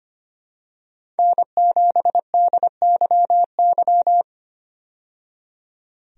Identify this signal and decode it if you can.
Morse code, heard as a keyed tone, about 25 wpm, 700 Hz. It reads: N7DYY
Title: USA Call Signs - 5 Characters - 25wpm